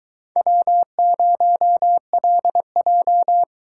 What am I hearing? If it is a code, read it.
Morse code, heard as a keyed tone, about 22 wpm, 700 Hz. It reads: W0LJ